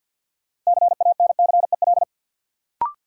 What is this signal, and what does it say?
Morse code, heard as a keyed tone, about 53 wpm, 700 Hz. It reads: CANCEL
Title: Sentences from Top 1200 Words - New Word Drill - 50wpm